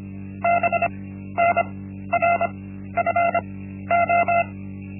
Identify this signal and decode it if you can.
Morse code, heard as a keyed tone, about 25 wpm, spaced as for 13 wpm, 690 Hz. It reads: BNRFO
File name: Linksmakalnis-1